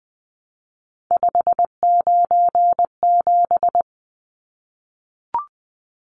Morse code, heard as a keyed tone, about 20 wpm, 700 Hz. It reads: 597
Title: Numbers - 3-Digit 20wpm